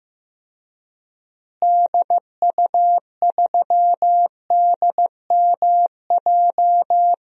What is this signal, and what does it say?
Morse code, heard as a keyed tone, about 15 wpm, 700 Hz. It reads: DU3DMJ
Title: DXCC-100 Call Signs - 6 Characters - Repeated 2x - 15wpm